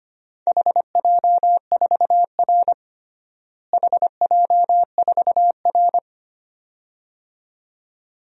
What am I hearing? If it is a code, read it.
Morse code, heard as a keyed tone, about 25 wpm, 700 Hz. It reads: HJ4R HJ4R
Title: DXCC-100 Call Signs - 4 Characters - Repeated 2x - 25wpm